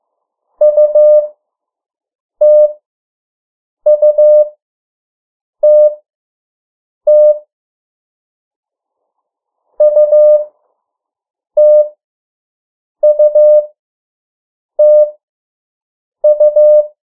Morse code, heard as a keyed tone, about 15 wpm, spaced as for 6 wpm, 600 Hz. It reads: UTUTT UTUTU
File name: morse mp3 / 03 urok 2